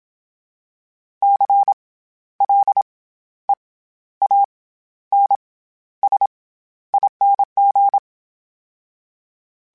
Morse code, heard as a keyed tone, about 27 wpm, 800 Hz. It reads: C L E A N S ING
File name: Special INGs-mp3s / Special INGs-mp3s / SlowToFaster ing @25